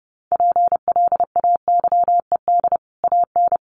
Morse code, heard as a keyed tone, about 28 wpm, 700 Hz. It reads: PLAYED AN
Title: Sets of 5 Words Encore2 30wpm